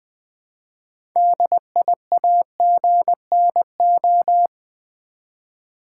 Morse code, heard as a keyed tone, about 20 wpm, 700 Hz. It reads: DIAGNO
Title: Build A Word - 3 Letters At A Time - 9-Letter Words - 20wpm